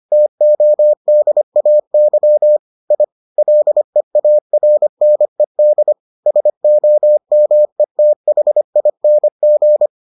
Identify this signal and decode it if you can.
Morse code, heard as a keyed tone, about 25 wpm, 600 Hz. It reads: TODAY I LEARNED SOMETHING